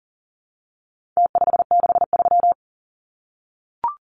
Morse code, heard as a keyed tone, about 40 wpm, 700 Hz. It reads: T563
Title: Numbers - Contest Style - 4-Digit 40wpm